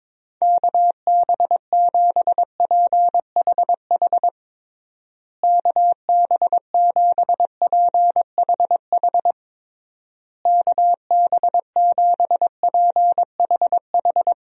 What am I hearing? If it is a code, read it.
Morse code, heard as a keyed tone, about 22 wpm, 700 Hz. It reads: KB7PHH KB7PHH KB7PHH
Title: USA Call Signs - 6 Characters - Repeated 3x - 22wpm